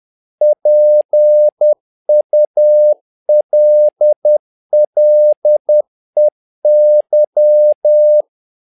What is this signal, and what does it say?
Morse code, heard as a keyed tone, about 10 wpm, 600 Hz. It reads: PULLEY
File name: learning center / tower trouble / tower trouble 10wpm